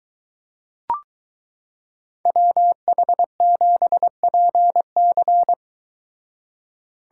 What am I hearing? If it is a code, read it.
Morse code, heard as a keyed tone, about 23 wpm, 700 Hz. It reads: WH7PC